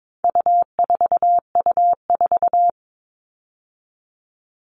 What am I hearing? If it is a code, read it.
Morse code, heard as a keyed tone, about 22 wpm, 700 Hz. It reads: U4U4
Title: All Cut Numbers - 4-Digit 22wpm